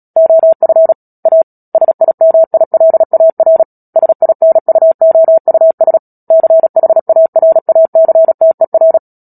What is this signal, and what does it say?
Morse code, heard as a keyed tone, about 37 wpm, 650 Hz. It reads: OF A SIMILAR SINUOUS CHARACTER
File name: Book Two / 10-Chapter 10 - Epilogue - 36 WPM